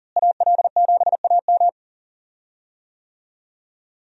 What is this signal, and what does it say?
Morse code, heard as a keyed tone, about 40 wpm, 700 Hz. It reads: AL7AM
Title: USA Call Signs - 5 Characters - Repeated 2x - 40wpm